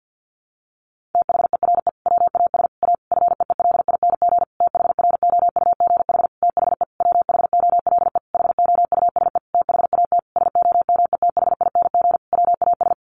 Morse code, heard as a keyed tone, about 50 wpm, 700 Hz. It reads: THERE WAS A FEELING THROUGH THE WHOLE HOUSE THAT SOMETHING WAS
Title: Sentences from Top 1200 Words 50wpm